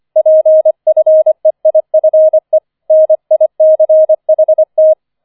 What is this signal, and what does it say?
Morse code, heard as a keyed tone, about 25 wpm, 630 Hz. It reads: PFEIFE NICHT
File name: 004 Vierter Streich 25wpm